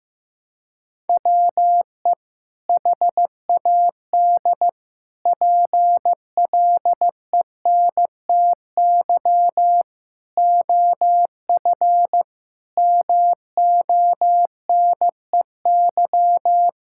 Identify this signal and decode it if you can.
Morse code, heard as a keyed tone, about 15 wpm, 700 Hz. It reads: WE HAD PLENTY OF MONEY
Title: Sentences from Top 1600 Words 15wpm